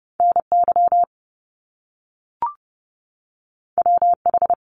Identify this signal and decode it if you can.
Morse code, heard as a keyed tone, about 30 wpm, 700 Hz. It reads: NY WH
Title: Top Two Letter Combinations in Words 30wpm